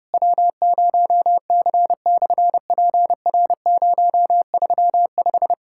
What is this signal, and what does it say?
Morse code, heard as a keyed tone, about 31 wpm, 700 Hz. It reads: W0C/PR035
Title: SOTA QSO Practice 30wpm